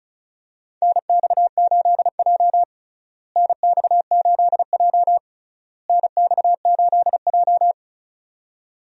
Morse code, heard as a keyed tone, about 36 wpm, 700 Hz. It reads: NX8J NX8J NX8J
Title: USA Call Signs - 4 Characters - Repeated 3x - 35wpm